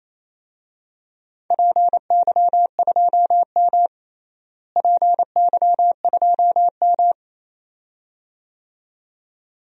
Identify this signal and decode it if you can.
Morse code, heard as a keyed tone, about 28 wpm, 700 Hz. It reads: PY2M PY2M
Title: DXCC-100 Call Signs - 4 Characters - Repeated 2x - 28wpm